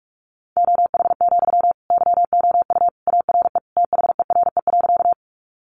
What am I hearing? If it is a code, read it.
Morse code, heard as a keyed tone, about 44 wpm, 700 Hz. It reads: OH, YOU ARE THERE.
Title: Sentences from Top 500 Words 45wpm